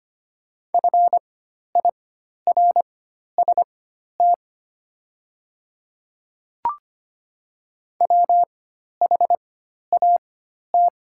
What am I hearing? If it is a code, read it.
Morse code, heard as a keyed tone, about 25 wpm, spaced as for 12 wpm, 700 Hz. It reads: FIRST WHAT
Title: 4x Character Spacing - Top 100 Words - 25wpm